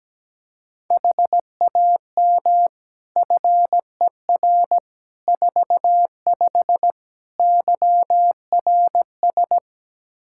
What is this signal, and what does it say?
Morse code, heard as a keyed tone, about 17 wpm, 700 Hz. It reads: HAM FER 45 YRS
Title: Practice QSO - Exchange 3 - More Info - 17wpm